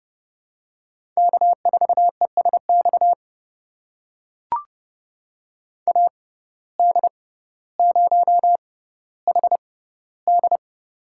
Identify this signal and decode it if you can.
Morse code, heard as a keyed tone, about 32 wpm, spaced as for 16 wpm, 700 Hz. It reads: K4ESX AD0HD
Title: USA Call Signs - 5 Characters - 6x Character Spacing - 30wpm